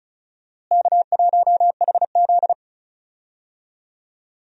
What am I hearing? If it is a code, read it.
Morse code, heard as a keyed tone, about 36 wpm, 700 Hz. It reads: K1HZ
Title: USA Call Signs - 4 Characters - Repeated 2x - 35wpm